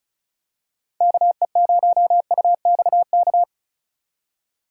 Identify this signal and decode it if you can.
Morse code, heard as a keyed tone, about 34 wpm, 700 Hz. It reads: KE0UXK
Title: USA Call Signs - 6 Characters - 4x Character Spacing - 35wpm